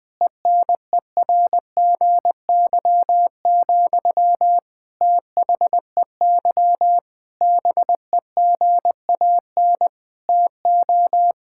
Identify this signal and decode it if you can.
Morse code, heard as a keyed tone, about 19 wpm, 700 Hz. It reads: ENERGY, THEY BEGAN TO
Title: Sentences from Top 1000 Words - Review Part 6 - 20wpm